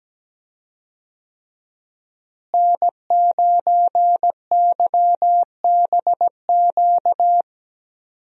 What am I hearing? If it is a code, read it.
Morse code, heard as a keyed tone, about 17 wpm, 700 Hz. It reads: N9YBQ
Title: USA Call Signs - 5 Characters - Repeated 3x - 17wpm